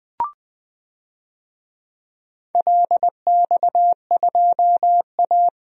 Morse code, heard as a keyed tone, about 21 wpm, 700 Hz. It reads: LX2A
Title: DXCC-100 Call Signs - 4 Characters - 20wpm